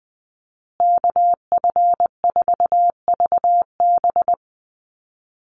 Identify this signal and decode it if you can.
Morse code, heard as a keyed tone, about 20 wpm, 700 Hz. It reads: KF4VB
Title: USA Call Signs Repeated 2x 20wpm